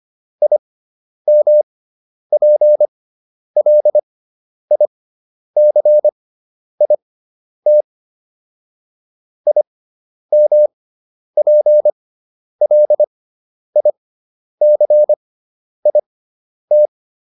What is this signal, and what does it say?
Morse code, heard as a keyed tone, about 25 wpm, spaced as for 10 wpm, 600 Hz. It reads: IMPLICIT IMPLICIT